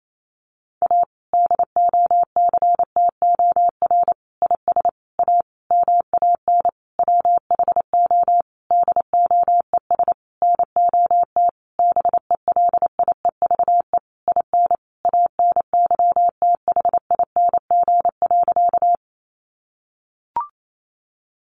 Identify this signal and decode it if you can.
Morse code, heard as a keyed tone, about 28 wpm, 700 Hz. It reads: A DOCTOR IS A MAN WHO DOES NOT BELIEVE IN ANYTHING.
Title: Sentences from Top 1500 Words 28wpm